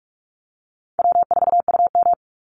Morse code, heard as a keyed tone, about 44 wpm, 700 Hz. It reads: W4UM
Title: Easy Call Signs - 45wpm